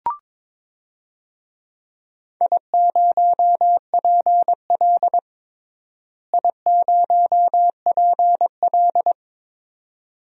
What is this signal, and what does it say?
Morse code, heard as a keyed tone, about 21 wpm, 700 Hz. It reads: I0PL I0PL
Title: DXCC-100 Call Signs - 4 Characters - Repeated 2x - 22wpm